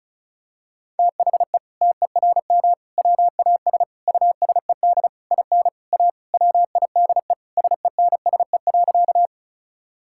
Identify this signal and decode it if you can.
Morse code, heard as a keyed tone, about 34 wpm, 700 Hz. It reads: THE TERM WAS USED IN A WIDE SENSE.
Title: Sentences from Top 1000 Words - Review Part 6 - 35wpm